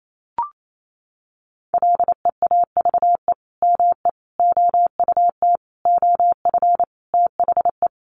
Morse code, heard as a keyed tone, about 28 wpm, 700 Hz. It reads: LEAVE ME OUT OF THE
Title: Sentences from Top 400 Words 28wpm